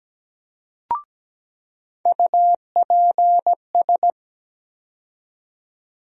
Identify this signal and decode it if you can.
Morse code, heard as a keyed tone, about 17 wpm, 700 Hz. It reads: UPS
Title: USA Call Sign Postfixes - 3 Letters 17wpm